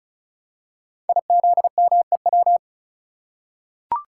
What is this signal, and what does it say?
Morse code, heard as a keyed tone, about 36 wpm, 700 Hz. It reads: IZMEW
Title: Code Groups - 5 Letters - 35wpm